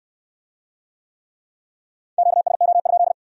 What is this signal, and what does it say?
Morse code, heard as a keyed tone, about 60 wpm, 700 Hz. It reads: QSO?